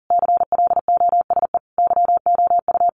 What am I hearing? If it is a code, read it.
Morse code, heard as a keyed tone, about 40 wpm, 700 Hz. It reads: CLOSE YOU
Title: A Sound Of Thunder 40wpm